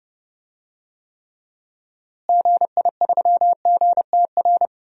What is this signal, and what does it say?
Morse code, heard as a keyed tone, about 32 wpm, 700 Hz. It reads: GI3GTR
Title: DXCC-100 Call Signs - 6 Characters - Repeated 2x - 30wpm